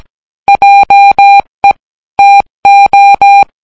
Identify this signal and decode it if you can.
Morse code, heard as a keyed tone, about 17 wpm, 800 Hz. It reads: JE TO